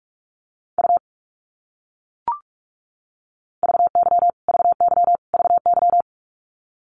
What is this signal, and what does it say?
Morse code, heard as a keyed tone, about 44 wpm, 700 Hz. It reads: U VY VY VY
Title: Abbreviations Repeated 3x - 45wpm